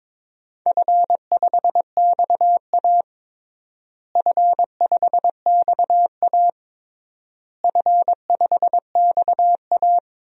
Morse code, heard as a keyed tone, about 22 wpm, 700 Hz. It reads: F5XA F5XA F5XA
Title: DXCC-100 Call Signs - 4 Characters - Repeated 3x - 22wpm